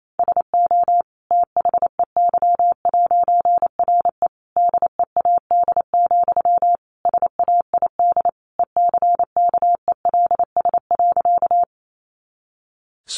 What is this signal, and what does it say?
Morse code, heard as a keyed tone, about 28 wpm, 700 Hz. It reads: SO THEY'RE DEAD, SAID ECKELS.
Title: A Sound Of Thunder 28wpm